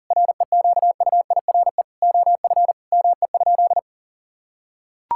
Morse code, heard as a keyed tone, about 40 wpm, 700 Hz. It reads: REQUIRE OF ME?
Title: Sentences from Top 1000 Words - Review Part 6 - 40wpm